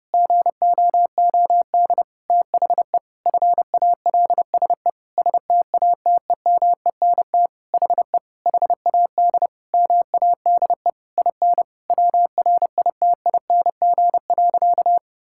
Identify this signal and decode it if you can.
Morse code, heard as a keyed tone, about 30 wpm, 700 Hz. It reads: GOOD THE FALSE STATEMENT HE HAD MADE IN WRITING.
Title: Sentences from Top 1600 Words 30wpm